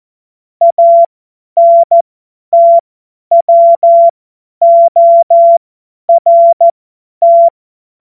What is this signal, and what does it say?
Morse code, heard as a keyed tone, about 14 wpm, spaced as for 10 wpm, 680 Hz. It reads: ANTWORT
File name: Morsekurs / Witze / Witz 21. Juni 2022 50 F70